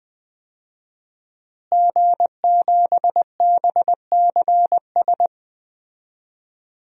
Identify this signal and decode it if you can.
Morse code, heard as a keyed tone, about 20 wpm, 700 Hz. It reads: G7BCS
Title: DXCC-100 Call Signs - 5 Characters - Repeated 3x - 20wpm